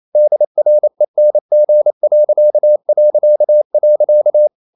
Morse code, heard as a keyed tone, about 28 wpm, 600 Hz. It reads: DRENG...
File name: Ferie QRQ 28wpm